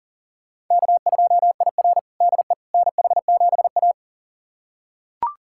Morse code, heard as a keyed tone, about 40 wpm, 700 Hz. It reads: K2IR DE NH7A
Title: USA Callsign DE Callsign - 4 Characters - 40wpm